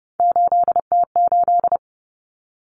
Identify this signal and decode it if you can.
Morse code, heard as a keyed tone, about 30 wpm, 700 Hz. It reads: 8T8
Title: Common Cut Numbers - 3-Digit 30wpm